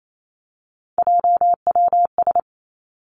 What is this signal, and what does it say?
Morse code, heard as a keyed tone, about 28 wpm, 700 Hz. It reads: JWS
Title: Three Challenging Letter-Number Combinations 28wpm